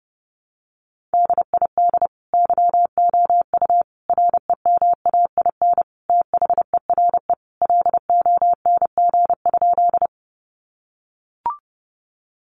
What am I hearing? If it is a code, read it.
Morse code, heard as a keyed tone, about 30 wpm, 700 Hz. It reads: DID YOU REMAIN THERE LONG?
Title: Sentences from Top 1200 Words 30wpm